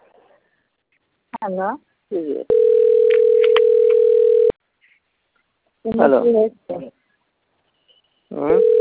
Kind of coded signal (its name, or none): telephone call progress tone